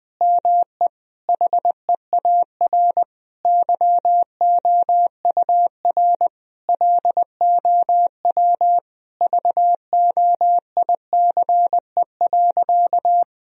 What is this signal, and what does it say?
Morse code, heard as a keyed tone, about 20 wpm, 700 Hz. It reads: ME HEAR YOUR LOW VOICE.
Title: Sentences from Top 1000 Words - Review Part 1 - 20wpm